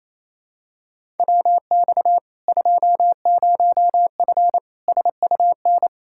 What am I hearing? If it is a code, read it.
Morse code, heard as a keyed tone, about 28 wpm, 700 Hz. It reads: WX 20F SUN
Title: Practice QSO - Exchange 2 - Info - 28wpm